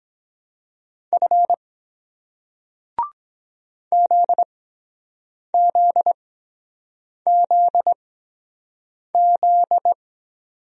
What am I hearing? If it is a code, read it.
Morse code, heard as a keyed tone, about 23 wpm, spaced as for 7 wpm, 700 Hz. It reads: F ZZZZ